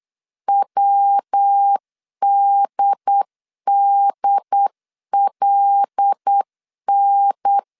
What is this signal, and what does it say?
Morse code, heard as a keyed tone, about 8 wpm, 800 Hz. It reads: WDDLN
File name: CW / DET / 1 6 1500C S C 8wpm 5219